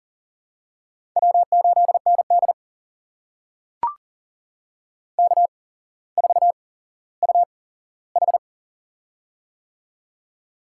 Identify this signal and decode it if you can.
Morse code, heard as a keyed tone, about 40 wpm, spaced as for 17 wpm, 700 Hz. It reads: W8ND K4UH
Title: USA Call Signs - 4 Characters - 8x Character Spacing - 40wpm